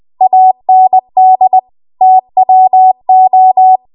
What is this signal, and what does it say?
Morse code, heard as a keyed tone, about 20 wpm, 750 Hz. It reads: AND TWO
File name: Morse / 20WPM